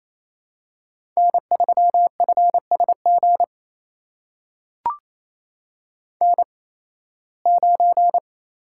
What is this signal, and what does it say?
Morse code, heard as a keyed tone, about 28 wpm, spaced as for 18 wpm, 700 Hz. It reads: N3FSG N9
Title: USA Call Signs - 5 Characters - 8x Character Spacing - 28wpm